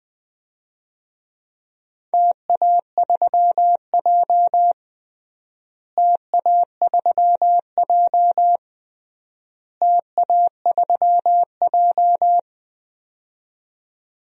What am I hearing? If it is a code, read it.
Morse code, heard as a keyed tone, about 20 wpm, 700 Hz. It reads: TA3J TA3J TA3J
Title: DXCC-100 Call Signs - 4 Characters - Repeated 3x - 20wpm